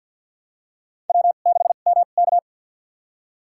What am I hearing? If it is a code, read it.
Morse code, heard as a keyed tone, about 47 wpm, spaced as for 35 wpm, 700 Hz. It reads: W6MK
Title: USA Call Signs - 2x Character Spacing - 50wpm